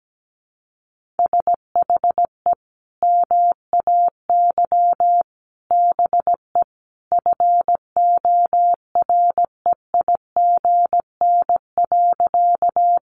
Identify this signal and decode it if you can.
Morse code, heard as a keyed tone, about 17 wpm, 700 Hz. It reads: SHE MAY BE FOREIGN.
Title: Sentences from Top 1400 Words 17wpm